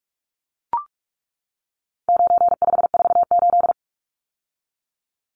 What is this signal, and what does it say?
Morse code, heard as a keyed tone, about 44 wpm, 700 Hz. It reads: 9548
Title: Numbers - 4-Digit 45wpm